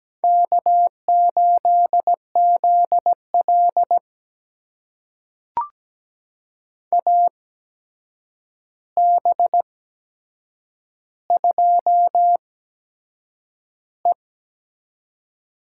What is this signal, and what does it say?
Morse code, heard as a keyed tone, about 17 wpm, spaced as for 7 wpm, 700 Hz. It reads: K8ZL AB2E
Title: USA Call Signs - 4 Characters - 8x Character Spacing - 17wpm